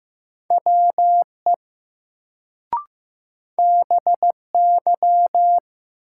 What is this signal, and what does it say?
Morse code, heard as a keyed tone, about 15 wpm, 700 Hz. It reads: WE BY
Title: Sentences from Top 100 Words - New Word Drill - 15wpm